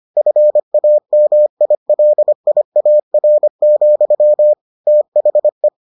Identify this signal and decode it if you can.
Morse code, heard as a keyed tone, about 25 wpm, 600 Hz. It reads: FAMILIAR, THE